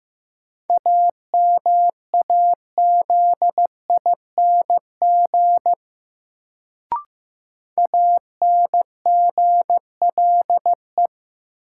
Morse code, heard as a keyed tone, about 15 wpm, 700 Hz. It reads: AMAZING ANGLE
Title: Sentences from Top 1500 Words - New Word Drill Alt - 15wpm